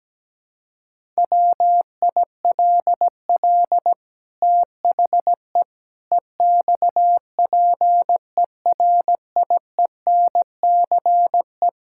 Morse code, heard as a keyed tone, about 17 wpm, 700 Hz. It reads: WILL THE EXPERIENCE
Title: Sentences from Top 1000 Words - Review Part 5 - 17wpm